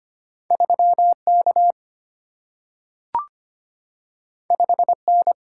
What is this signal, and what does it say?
Morse code, heard as a keyed tone, about 25 wpm, 700 Hz. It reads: 3 K 5 N
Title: Single-Letter Single-Number 25wpm